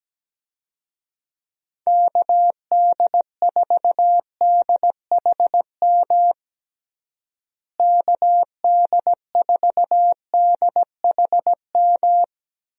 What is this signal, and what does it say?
Morse code, heard as a keyed tone, about 17 wpm, 700 Hz. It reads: KD4DHM KD4DHM
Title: USA Call Signs - 6 Characters - Repeated 2x - 17wpm